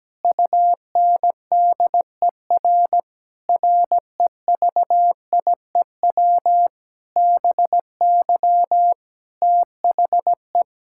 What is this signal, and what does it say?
Morse code, heard as a keyed tone, about 17 wpm, 700 Hz. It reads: UNDER REVIEW BY THE
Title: Sentences from Top 1000 Words - Review Part 6 - 17wpm